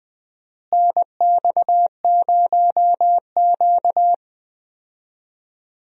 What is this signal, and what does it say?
Morse code, heard as a keyed tone, about 20 wpm, 700 Hz. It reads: NX0Q
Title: USA Call Signs - 4 Characters - Repeated 3x - 20wpm